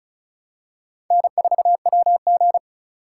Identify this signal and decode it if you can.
Morse code, heard as a keyed tone, about 36 wpm, 700 Hz. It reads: N4WG
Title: USA Call Signs - 4 Characters - 4x Character Spacing - 35wpm